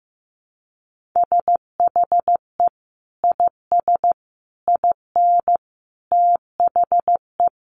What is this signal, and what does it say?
Morse code, heard as a keyed tone, about 15 wpm, 700 Hz. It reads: SHE IS IN THE
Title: Sentences from Top 1200 Words 15wpm